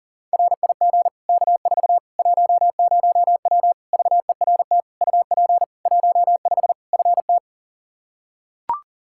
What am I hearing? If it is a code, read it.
Morse code, heard as a keyed tone, about 40 wpm, 700 Hz. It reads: RIG K4 10W VERT UP 15 FT
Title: Practice QSO - Exchange 2 - Info - 40wpm